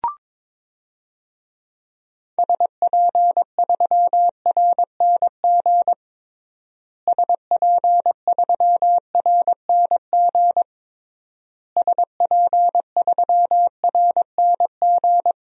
Morse code, heard as a keyed tone, about 22 wpm, 700 Hz. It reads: SP3RNG SP3RNG SP3RNG
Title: DXCC-100 Call Signs - 6 Characters - Repeated 3x - 22wpm